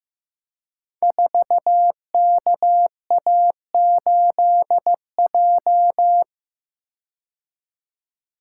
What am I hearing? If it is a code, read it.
Morse code, heard as a keyed tone, about 15 wpm, 700 Hz. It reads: 4KA8J
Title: Code Groups - 5 Letter:Numbers - 15wpm